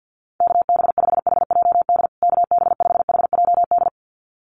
Morse code, heard as a keyed tone, about 48 wpm, 700 Hz. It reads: KB5HJD KB5HJD
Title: USA Call Signs Repeated 2x 50wpm